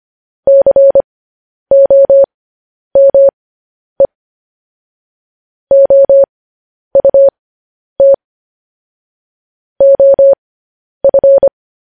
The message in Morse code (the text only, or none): COME OUT OF